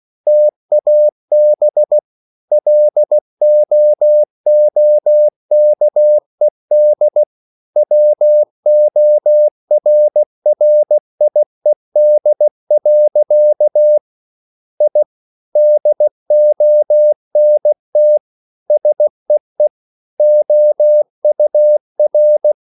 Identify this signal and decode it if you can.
Morse code, heard as a keyed tone, about 16 wpm, 600 Hz. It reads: TAB LOOKED WORRIED. I DONT SEE OUR